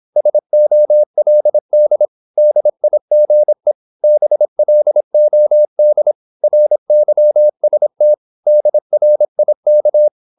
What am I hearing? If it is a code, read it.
Morse code, heard as a keyed tone, about 26 wpm, 600 Hz. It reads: SOLD DIGE BLOD RYST DRIK
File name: Korte ord DK 26wpm